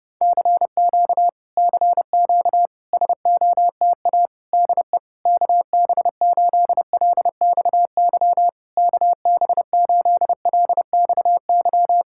Morse code, heard as a keyed tone, about 29 wpm, 700 Hz. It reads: CQ CQ SOTA DE KB8LXY KB8LXY
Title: Speed-Racing Common QSO Phrases Encore - 30wpm